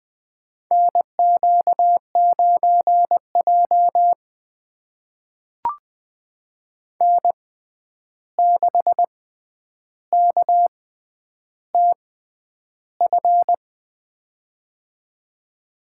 Morse code, heard as a keyed tone, about 20 wpm, spaced as for 10 wpm, 700 Hz. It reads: NQ9J N6KTF
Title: USA Call Signs - 6x Character Spacing - 20wpm